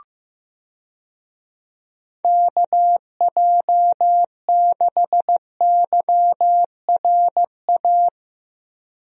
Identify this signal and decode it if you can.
Morse code, heard as a keyed tone, about 15 wpm, 700 Hz. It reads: KJ6YRA
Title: USA Call Signs - 6 Characters - 15wpm